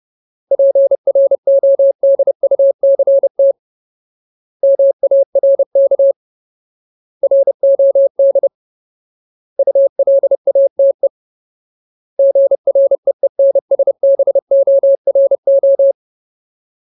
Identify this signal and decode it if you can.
Morse code, heard as a keyed tone, about 30 wpm, 550 Hz. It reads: PRODUCT MARK ROD ULATE GREENSBORO
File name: Ultimate Word List / 30WPM Ultimate